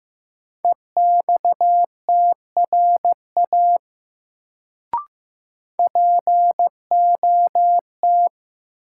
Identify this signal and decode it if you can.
Morse code, heard as a keyed tone, about 15 wpm, 700 Hz. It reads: EXTRA POT
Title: Sentences from Top 1000 Words - New Word Drill - 15wpm